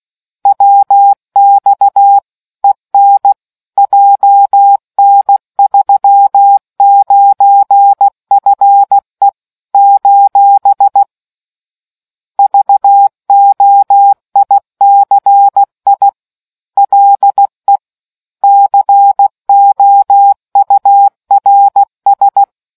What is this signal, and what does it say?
Morse code, heard as a keyed tone, about 16 wpm, 800 Hz. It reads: WX EN JN39FE : VOICI LE COURS